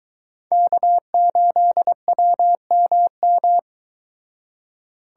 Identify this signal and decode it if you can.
Morse code, heard as a keyed tone, about 22 wpm, 700 Hz. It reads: K8WMM